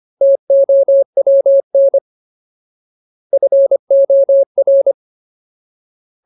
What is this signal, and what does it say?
Morse code, heard as a keyed tone, about 25 wpm, 550 Hz. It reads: TOWN FOR
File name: Ultimate Word List / 25WPM Ultimate